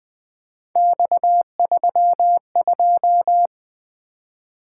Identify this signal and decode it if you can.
Morse code, heard as a keyed tone, about 20 wpm, 700 Hz. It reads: X32
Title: Three Letter-Number Combinations - 20wpm